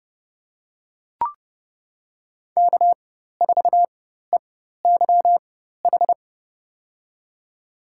Morse code, heard as a keyed tone, about 30 wpm, spaced as for 14 wpm, 700 Hz. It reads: K4EYH
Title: USA Call Signs - 5 Characters - 4x Character Spacing - 30wpm